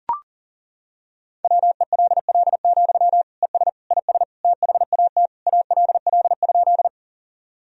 Morse code, heard as a keyed tone, about 40 wpm, 700 Hz. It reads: WELL, ES IS THAT ALL?
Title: Sentences from Top 200 Words and Abbreviations 40wpm